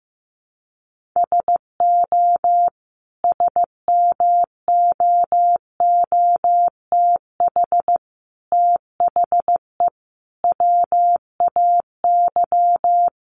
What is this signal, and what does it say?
Morse code, heard as a keyed tone, about 15 wpm, 700 Hz. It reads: SO SMOOTH THE WAY
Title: Sentences from Top 1400 Words 15wpm